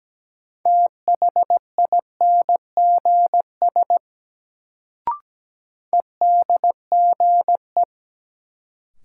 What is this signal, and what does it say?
Morse code, heard as a keyed tone, about 17 wpm, 700 Hz. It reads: THINGS EDGE
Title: Sets of 3 Words - New Word Drill - 17wpm